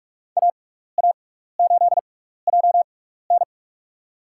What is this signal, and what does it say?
Morse code, heard as a keyed tone, about 46 wpm, spaced as for 16 wpm, 700 Hz. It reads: AA8JN
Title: USA Call Signs - 5 Characters - 6x Character Spacing - 45wpm